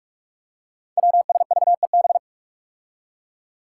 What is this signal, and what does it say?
Morse code, heard as a keyed tone, about 44 wpm, 700 Hz. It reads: WSVEB
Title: Five Challenging Letter-Number Combinations 45wpm